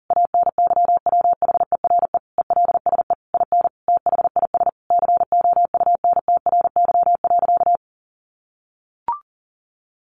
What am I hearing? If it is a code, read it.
Morse code, heard as a keyed tone, about 40 wpm, 700 Hz. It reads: ANYWHERE ELSE IN THIS COUNTRY.
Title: Sentences from Top 1400 Words 40wpm